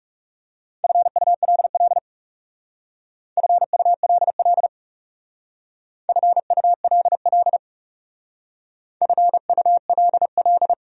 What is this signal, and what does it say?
Morse code, heard as a keyed tone, about 37 wpm, 700 Hz. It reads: FULL FULL FULL FULL
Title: Speed-Racing Top 500 Words 30wpm